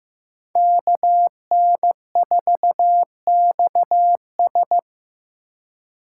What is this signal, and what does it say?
Morse code, heard as a keyed tone, about 15 wpm, 700 Hz. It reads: KN4XS
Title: USA Call Signs - 5 Characters - 4x Character Spacing - 15wpm